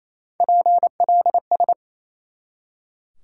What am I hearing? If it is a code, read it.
Morse code, heard as a keyed tone, about 28 wpm, 700 Hz. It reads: PLS
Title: Top 180 QSO Elements 28wpm